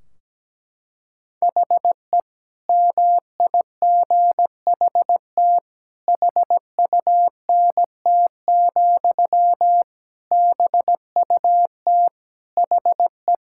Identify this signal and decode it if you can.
Morse code, heard as a keyed tone, about 17 wpm, 700 Hz. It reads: HE MIGHT HUNT, BUT HE
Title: Sentences from Top 1600 Words 17wpm